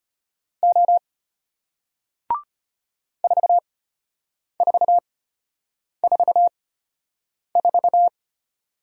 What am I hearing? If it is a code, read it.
Morse code, heard as a keyed tone, about 33 wpm, spaced as for 8 wpm, 700 Hz. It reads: O 4444